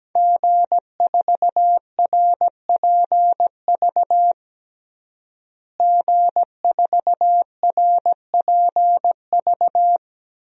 Morse code, heard as a keyed tone, about 17 wpm, 700 Hz. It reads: G4RPV G4RPV
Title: DXCC-100 Call Signs - 5 Characters - Repeated 3x - 17wpm